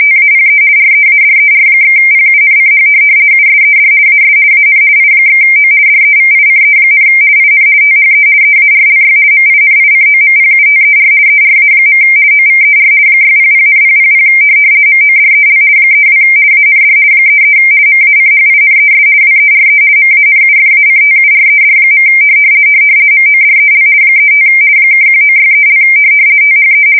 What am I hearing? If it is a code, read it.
RTTY, about 50 baud. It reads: TAPE READER/PUNCH USED AS AN I/O DEVICE, IN THE EARLY MINICOMPUTER ERA. OVER 600,000 MODEL 32 AND 33 MACHINES WERE MANUFACTURED. MODEL 35 (1963) THE
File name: fsk170-50B / teletype-corp